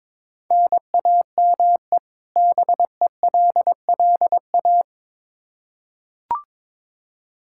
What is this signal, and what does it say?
Morse code, heard as a keyed tone, about 22 wpm, 700 Hz. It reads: NAME BELLA
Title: Practice QSO - Exchange 1 - Beginning - 22wpm